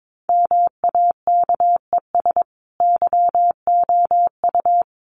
Morse code, heard as a keyed tone, about 22 wpm, 700 Hz. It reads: MAKES YOU
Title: A Sound Of Thunder 22wpm